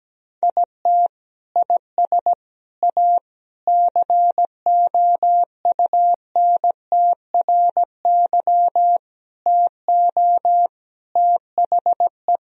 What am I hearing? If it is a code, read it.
Morse code, heard as a keyed tone, about 17 wpm, 700 Hz. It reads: IT IS A COUNTRY TO THE
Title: Sentences from Top 1000 Words - Review Part 1 - 17wpm